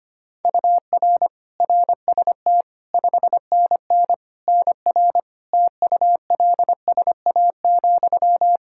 Morse code, heard as a keyed tone, about 25 wpm, 700 Hz. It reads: UR RST 5NN NR TULSA,
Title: Practice QSO - Exchange 1 - Beginning - 25wpm